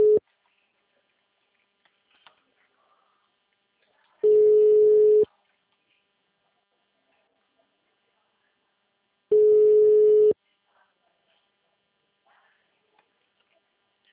Ringback tone (Europe, 425 Hz).
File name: out- -2019- - - .59340